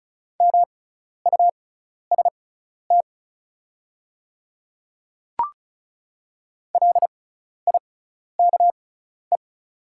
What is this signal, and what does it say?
Morse code, heard as a keyed tone, about 36 wpm, spaced as for 12 wpm, 690 Hz. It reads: MUST LIKE
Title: 6x Character Spacing - Top 200 Words - 35wpm